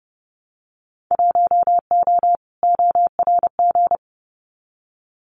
Morse code, heard as a keyed tone, about 30 wpm, 700 Hz. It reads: 1O ORG
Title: Winter Field Day Exchanges 30wpm